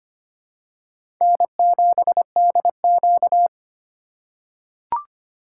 Morse code, heard as a keyed tone, about 25 wpm, 700 Hz. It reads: N7DQ
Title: USA Call Signs - 4 Characters - 4x Character Spacing - 25wpm